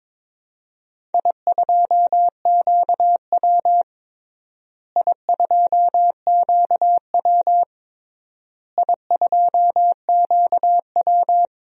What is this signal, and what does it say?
Morse code, heard as a keyed tone, about 23 wpm, 700 Hz. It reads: I2QW I2QW I2QW
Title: DXCC-100 Call Signs - 4 Characters - Repeated 3x - 22wpm